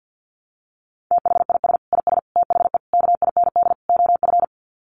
Morse code, heard as a keyed tone, about 50 wpm, 700 Hz. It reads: THIS IS THE KIND OF
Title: Sets of 5 Words Encore 50wpm